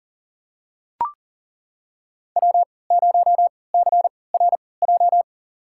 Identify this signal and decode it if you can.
Morse code, heard as a keyed tone, about 40 wpm, spaced as for 23 wpm, 700 Hz. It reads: W0CRJ
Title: USA Call Signs - 5 Characters - 3x Character Spacing - 40wpm